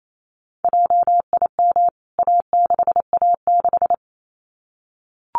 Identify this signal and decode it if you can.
Morse code, heard as a keyed tone, about 28 wpm, 700 Hz. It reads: JIM A6A6
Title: CWT Exchange 28wpm